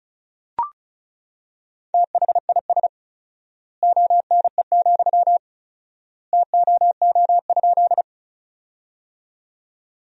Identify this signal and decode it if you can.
Morse code, heard as a keyed tone, about 36 wpm, 700 Hz. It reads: THIS ONE, TOO?
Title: 4x Word-Spacing - Sentences from Top 200 Words - 35wpm